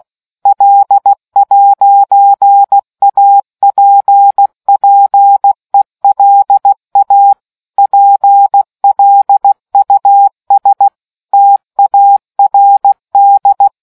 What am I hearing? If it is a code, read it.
Morse code, heard as a keyed tone, about 16 wpm, 800 Hz. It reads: L'APPELA PLUS TARD